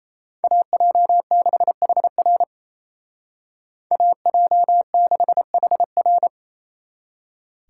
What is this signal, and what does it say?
Morse code, heard as a keyed tone, about 32 wpm, 700 Hz. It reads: AJ6HR AJ6HR
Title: Speed Racing - USA 2x2 Call Signs 28wpm